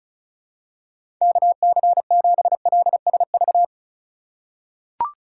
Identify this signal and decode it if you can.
Morse code, heard as a keyed tone, about 34 wpm, 700 Hz. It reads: KC7LSV
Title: USA Call Signs - 6 Characters - 2x Character Spacing - 35wpm